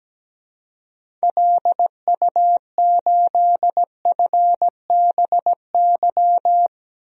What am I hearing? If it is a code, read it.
Morse code, heard as a keyed tone, about 17 wpm, 700 Hz. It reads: LU8FBY
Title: DXCC-100 Call Signs - 6 Characters - 17wpm